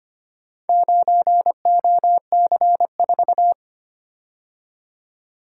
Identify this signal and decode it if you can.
Morse code, heard as a keyed tone, about 25 wpm, 700 Hz. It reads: 9OC4
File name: practice / 25wpm / Four Letter-Number Combinations - 25wpm